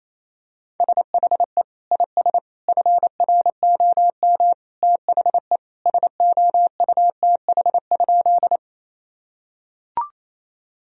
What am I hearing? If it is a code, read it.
Morse code, heard as a keyed tone, about 28 wpm, 700 Hz. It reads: SHE IS FROM THE SOUTH?
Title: Sentences from Top 1000 Words - Review Part 2 - 28wpm